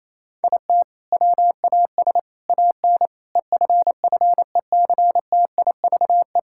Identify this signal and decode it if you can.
Morse code, heard as a keyed tone, about 28 wpm, 700 Hz. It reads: IT WAS AN EFFECTIVE
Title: Sentences from Top 1000 Words - Review Part 6 - 28wpm